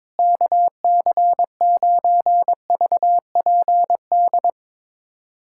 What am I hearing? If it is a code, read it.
Morse code, heard as a keyed tone, about 22 wpm, 700 Hz. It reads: KC9VPD
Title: USA Callsign DE Callsign - 6 Characters - 22wpm